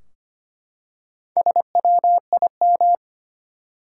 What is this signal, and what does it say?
Morse code, heard as a keyed tone, about 25 wpm, 700 Hz. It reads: SWIM
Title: Four Letter Words - 25wpm